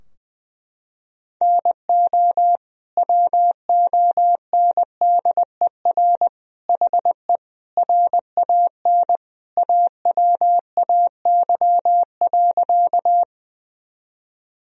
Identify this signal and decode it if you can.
Morse code, heard as a keyed tone, about 20 wpm, 700 Hz. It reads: NO WONDER HE RAN AWAY.
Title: Sentences from Top 1000 Words - Review Part 3 - 20wpm